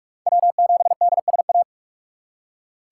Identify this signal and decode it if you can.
Morse code, heard as a keyed tone, about 41 wpm, 700 Hz. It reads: W7DSA